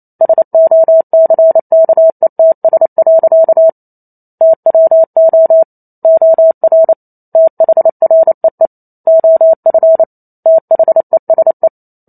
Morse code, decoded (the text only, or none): SOCKETS. TWO OR THREE OF THESE